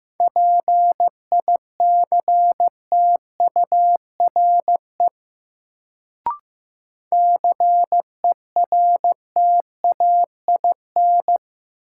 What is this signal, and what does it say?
Morse code, heard as a keyed tone, about 15 wpm, 700 Hz. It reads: PICTURE CERTAIN
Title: Sentences from Top 500 Words - New Word Drill - 15wpm